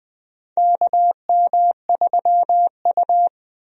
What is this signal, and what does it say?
Morse code, heard as a keyed tone, about 20 wpm, 700 Hz. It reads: KM3U
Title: USA Call Signs - 4 Characters - Repeated 2x - 20wpm